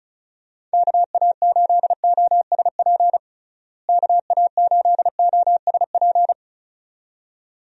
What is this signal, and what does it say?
Morse code, heard as a keyed tone, about 36 wpm, 700 Hz. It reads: KA8OSP KA8OSP
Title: USA Call Signs - 6 Characters - Repeated 2x - 35wpm